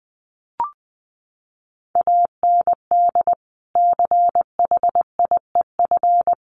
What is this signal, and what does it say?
Morse code, heard as a keyed tone, about 20 wpm, 700 Hz. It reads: AND CHIEF
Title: Sets of 5 Words Encore2 20wpm